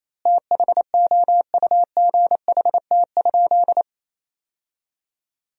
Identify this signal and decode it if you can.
Morse code, heard as a keyed tone, about 27 wpm, 700 Hz. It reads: THOUGHT?
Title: 3x Word-Spacing - Sentences from Top 300 Words - 28wpm